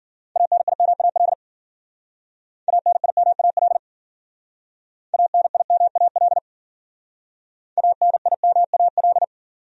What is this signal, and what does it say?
Morse code, heard as a keyed tone, about 51 wpm, 700 Hz. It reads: ANIMAL ANIMAL ANIMAL ANIMAL